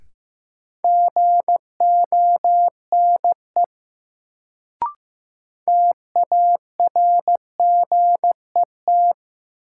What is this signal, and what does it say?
Morse code, heard as a keyed tone, about 15 wpm, 700 Hz. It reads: GONE TARGET
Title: Sentences from Top 1200 Words - New Word Drill - 15wpm